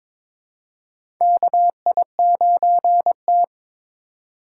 Morse code, heard as a keyed tone, about 22 wpm, 700 Hz. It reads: KI9T
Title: USA Call Signs - 4 Characters - 3x Character Spacing - 22wpm